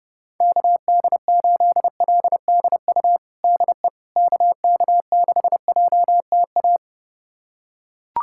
Morse code, heard as a keyed tone, about 30 wpm, 700 Hz. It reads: KD8LDU DE KK6JTA
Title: USA Callsign DE Callsign - 6 Characters - 30wpm